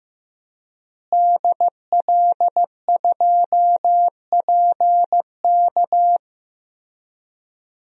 Morse code, heard as a keyed tone, about 15 wpm, 700 Hz. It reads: DL2PK
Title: DXCC-100 Call Signs - 5 Characters - Repeated 3x - 15wpm